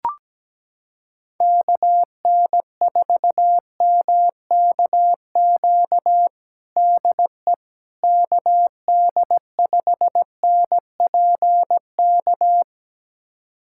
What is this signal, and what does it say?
Morse code, heard as a keyed tone, about 17 wpm, 700 Hz. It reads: KN4MKQ DE KD5NPK
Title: USA Callsign DE Callsign - 6 Characters - 17wpm